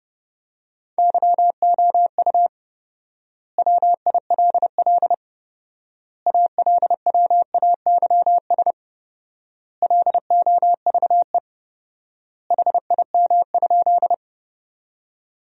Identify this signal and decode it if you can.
Morse code, heard as a keyed tone, about 32 wpm, 700 Hz. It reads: YOU WILL ALWAYS LOVE HIM?
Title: 4x Word-Spacing - Sentences from Top 400 Words - 30wpm